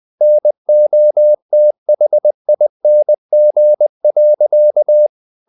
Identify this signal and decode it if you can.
Morse code, heard as a keyed tone, about 20 wpm, 600 Hz. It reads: NOTHING.
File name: the old man and the sea / the old man and the sea part 1 20wpm